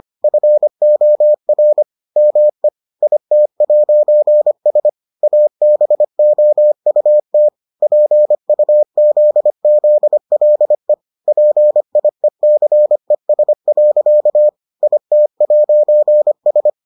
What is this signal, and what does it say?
Morse code, heard as a keyed tone, about 25 wpm, 600 Hz. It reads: FOR ME IT'S ABOUT PUZZLE PIECES. IT'S